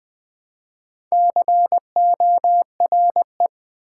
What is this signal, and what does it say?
Morse code, heard as a keyed tone, about 20 wpm, 700 Hz. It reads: CORE
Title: Build A Word - 1 Letter At A Time - 4-Letter Words - 20wpm